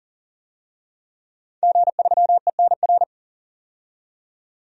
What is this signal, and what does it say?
Morse code, heard as a keyed tone, about 40 wpm, 700 Hz. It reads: G3ENR
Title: DXCC-100 Call Signs - 5 Characters - 40wpm